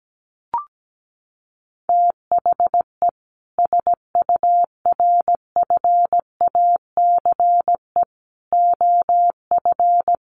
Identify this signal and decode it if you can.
Morse code, heard as a keyed tone, about 17 wpm, 700 Hz. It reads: THE SURFACE OF
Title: Sets of 4 Words 17wpm